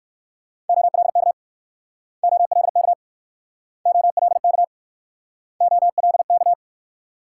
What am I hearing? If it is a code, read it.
Morse code, heard as a keyed tone, about 62 wpm, 700 Hz. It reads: OLK OLK OLK OLK